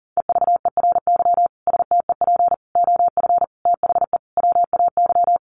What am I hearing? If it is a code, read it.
Morse code, heard as a keyed tone, about 40 wpm, 700 Hz. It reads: EVERY STEP OF THE WAY
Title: Sets of 5 Words Encore2 40wpm